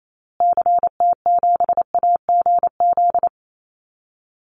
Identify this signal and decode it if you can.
Morse code, heard as a keyed tone, about 28 wpm, 700 Hz. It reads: CT7AGZ
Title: Call Signs Repeated 2x - 28wpm